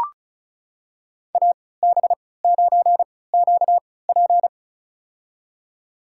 Morse code, heard as a keyed tone, about 34 wpm, spaced as for 20 wpm, 700 Hz. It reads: AB9QP
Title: USA Call Signs - 5 Characters - 3x Character Spacing - 35wpm